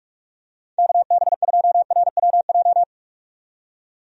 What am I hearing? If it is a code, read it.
Morse code, heard as a keyed tone, about 44 wpm, 700 Hz. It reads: KB2RWJ
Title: USA Call Signs - 6 Characters - 45wpm